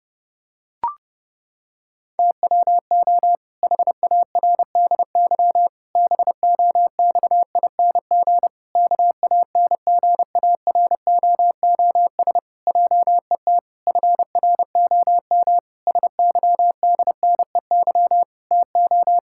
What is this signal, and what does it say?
Morse code, heard as a keyed tone, about 30 wpm, 700 Hz. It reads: TWO HARDY BOXING KANGAROOS JET FROM SYDNEY TO
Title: Pangram Challenge v2 - 30wpm